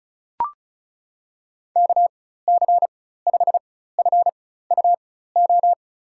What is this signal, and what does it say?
Morse code, heard as a keyed tone, about 34 wpm, spaced as for 16 wpm, 700 Hz. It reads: KC5FUO
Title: USA Call Signs - 4x Character Spacing - 35wpm